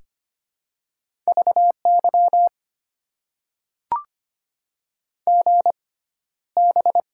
Morse code, heard as a keyed tone, about 25 wpm, spaced as for 13 wpm, 690 Hz. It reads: VY GB
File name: Top 90 QSO Elements - 6x Character Spacing - 25wpm